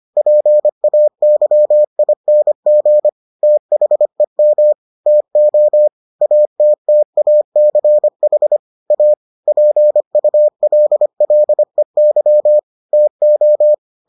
Morse code, heard as a keyed tone, about 25 wpm, 600 Hz. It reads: PAYING THEM TO ATTACH A PULLEY TO